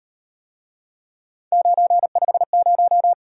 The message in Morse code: 950

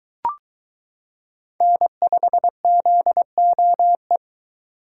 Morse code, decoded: N5ZOE

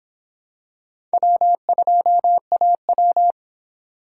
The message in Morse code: W2AW

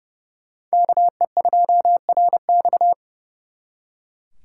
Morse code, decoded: KE2RX

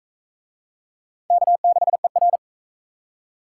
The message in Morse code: K6ER

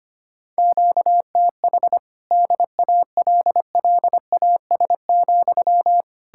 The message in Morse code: QTH DALLAS,

